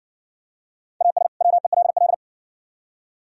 Morse code, heard as a keyed tone, about 60 wpm, 700 Hz. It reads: AS WELL